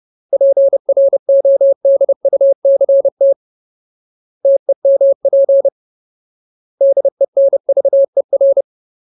Morse code, heard as a keyed tone, about 30 wpm, 550 Hz. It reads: PRODUCT TEMP DENVER
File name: Ultimate Word List / 30WPM Ultimate